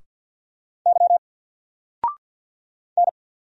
Morse code, heard as a keyed tone, about 48 wpm, 700 Hz. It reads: Y N
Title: Single Letter-Number Alt 50wpm